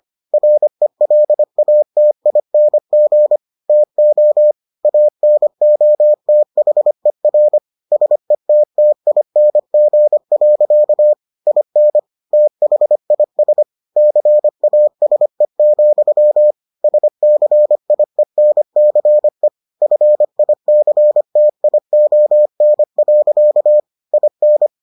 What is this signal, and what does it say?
Morse code, heard as a keyed tone, about 25 wpm, 600 Hz. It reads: RELATING TO ANOTHER SETTING. IN THIS CASE, SCIENCE FICTION. IN